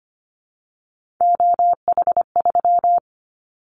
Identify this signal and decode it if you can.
Morse code, heard as a keyed tone, about 25 wpm, 700 Hz. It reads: OH3
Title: Callsign Prefixes for Top 50 DXCC - 25wpm